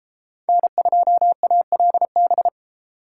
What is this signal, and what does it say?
Morse code, heard as a keyed tone, about 40 wpm, 700 Hz. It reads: N2ALB